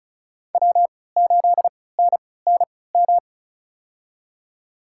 Morse code, heard as a keyed tone, about 34 wpm, spaced as for 20 wpm, 700 Hz. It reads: W8NNM
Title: USA Call Signs - 5 Characters - 3x Character Spacing - 35wpm